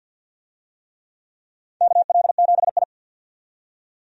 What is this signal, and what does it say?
Morse code, heard as a keyed tone, about 47 wpm, 700 Hz. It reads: KL7I
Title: DXCC-100 Call Signs - 4 Characters - 50wpm